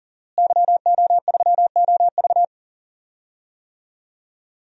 Morse code, heard as a keyed tone, about 40 wpm, 700 Hz. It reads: YO3OV
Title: DXCC-100 Call Signs - 5 Characters - Repeated 3x - 40wpm